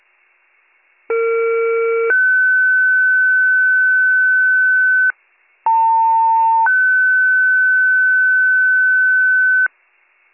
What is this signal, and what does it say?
Two-tone page, heard as a one-second tone, then a three-second tone, about 460 then 1500 Hz, about 900 then 1500 Hz. It is heard from 1.1 to 9.7 s.